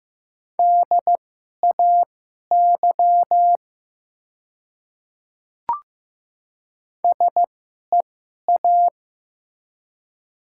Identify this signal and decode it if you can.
Morse code, heard as a keyed tone, about 15 wpm, spaced as for 11 wpm, 700 Hz. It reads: DAY SEA
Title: 2x Character Spacing - Top 500 Words - 15wpm